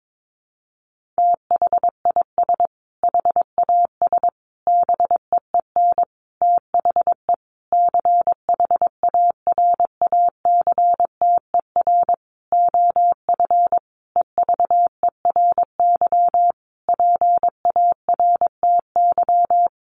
Morse code, heard as a keyed tone, about 22 wpm, 700 Hz. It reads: THIS HAS BEEN THE CHARACTER OF EVERY PARTY
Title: Sentences from Top 1400 Words 22wpm